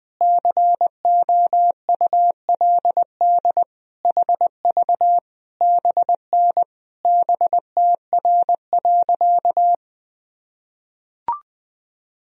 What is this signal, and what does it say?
Morse code, heard as a keyed tone, about 20 wpm, 700 Hz. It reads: COULD HV BN BTR.
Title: Sentences from Top 500 Words and Abbreviations 20wpm